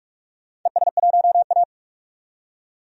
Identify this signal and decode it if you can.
Morse code, heard as a keyed tone, about 46 wpm, 700 Hz. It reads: ES1A
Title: DXCC-100 Call Signs - 4 Characters - 45wpm